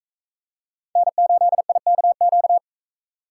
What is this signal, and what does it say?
Morse code, heard as a keyed tone, about 40 wpm, 700 Hz. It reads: N8IKQ